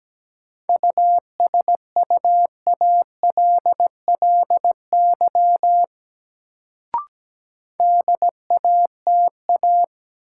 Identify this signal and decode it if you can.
Morse code, heard as a keyed tone, about 17 wpm, 700 Hz. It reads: USUALLY DATA
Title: Sentences from Top 1000 Words - Review Part 3 - New Word Drill - 17wpm